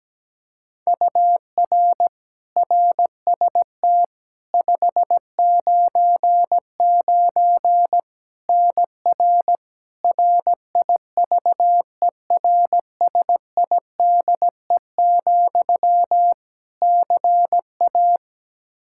Morse code, heard as a keyed tone, about 17 wpm, 700 Hz. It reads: UR RST 599 NR RIVERSIDE, CA
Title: Practice QSO - Exchange 1 - Beginning - 17wpm